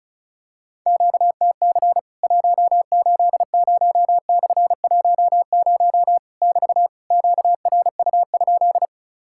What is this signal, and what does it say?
Morse code, heard as a keyed tone, about 34 wpm, 700 Hz. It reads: QTC 180/10 <BT> QRU?
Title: Worked All Europe DX Contest - QTC Exchange - 35wpm